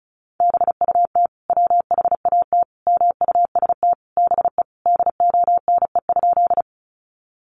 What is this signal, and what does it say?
Morse code, heard as a keyed tone, about 34 wpm, 700 Hz. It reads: BUT WHAT MUST BE DONE?
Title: Sentences from Top 500 Words 35wpm